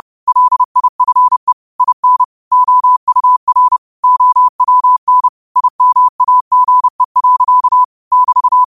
Morse code, heard as a keyed tone, about 30 wpm, 1000 Hz. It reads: LIFE IN OUR OWN IMAGE. <BT>
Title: 30 WPM morse code quotes for Fri, 01 Mar 2024